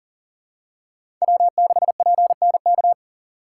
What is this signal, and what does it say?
Morse code, heard as a keyed tone, about 40 wpm, 700 Hz. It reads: W6PNK
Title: USA Call Signs - 5 Characters - 3x Character Spacing - 40wpm